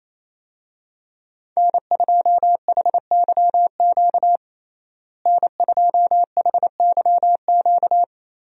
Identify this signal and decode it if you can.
Morse code, heard as a keyed tone, about 28 wpm, 700 Hz. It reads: N2HYQ N2HYQ
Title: USA Call Signs - 5 Characters - Repeated 2x - 28wpm